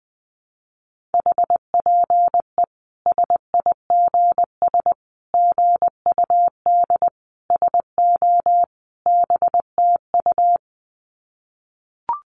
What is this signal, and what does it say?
Morse code, heard as a keyed tone, about 20 wpm, 700 Hz. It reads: HPE SIGS GUD SO BTU
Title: Common QSO Phrases 20wpm